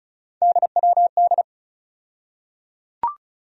Morse code, heard as a keyed tone, about 36 wpm, 700 Hz. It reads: DWD
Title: USA Call Sign Postfixes - 3 Letters 35wpm